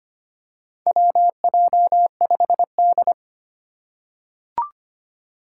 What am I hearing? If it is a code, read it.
Morse code, heard as a keyed tone, about 25 wpm, 700 Hz. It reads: WJ5D
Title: USA Call Signs - 4 Characters - 3x Character Spacing - 25wpm